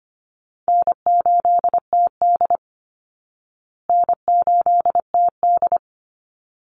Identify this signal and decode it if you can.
Morse code, heard as a keyed tone, about 25 wpm, 700 Hz. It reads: N8TD N8TD
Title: Call Signs Repeated 3x - 25wpm